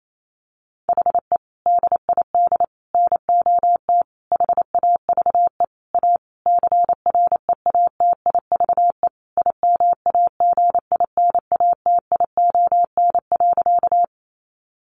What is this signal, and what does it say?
Morse code, heard as a keyed tone, about 28 wpm, 700 Hz. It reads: HE DID NOT HAVE A CREATIVE IMAGINATION.
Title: Sentences from Top 1300 Words 28wpm